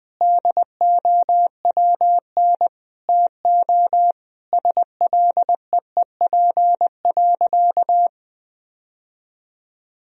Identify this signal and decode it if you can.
Morse code, heard as a keyed tone, about 20 wpm, 700 Hz. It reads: DOWN TO SLEEP.
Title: Sentences from Top 1000 Words - Review Part 2 - 20wpm